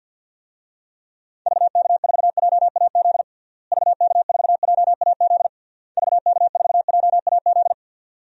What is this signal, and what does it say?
Morse code, heard as a keyed tone, about 50 wpm, 700 Hz. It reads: VK4JAZ VK4JAZ VK4JAZ
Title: DXCC-100 Call Signs - 6 Characters - Repeated 3x - 50wpm